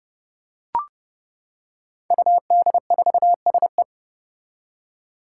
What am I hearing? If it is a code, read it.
Morse code, heard as a keyed tone, about 30 wpm, 700 Hz. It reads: UD4SE
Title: Five Challenging Letter-Number Combinations 30wpm